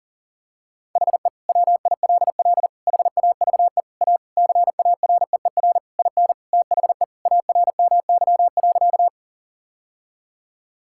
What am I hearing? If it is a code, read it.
Morse code, heard as a keyed tone, about 40 wpm, 700 Hz. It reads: HE WILL HAVE A CAREER IN THE ARMY.
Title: Sentences from Top 1000 Words - Review Part 6 - 40wpm